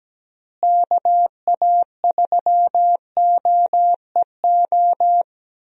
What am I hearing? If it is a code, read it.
Morse code, heard as a keyed tone, about 17 wpm, 700 Hz. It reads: KA3OEO
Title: USA Call Signs - 6 Characters - Repeated 2x - 17wpm